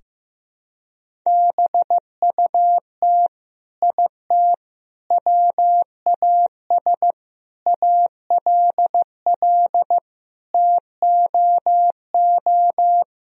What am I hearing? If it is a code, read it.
Morse code, heard as a keyed tone, about 16 wpm, 700 Hz. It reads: BUT IT WAS ALL TOO
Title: Sentences from Top 1000 Words - Review Part 4 - 15wpm